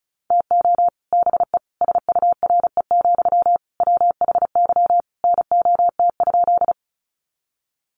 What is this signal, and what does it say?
Morse code, heard as a keyed tone, about 34 wpm, 700 Hz. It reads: TO BE SURE, WHY NOT?
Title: Sentences from Top 600 Words 35wpm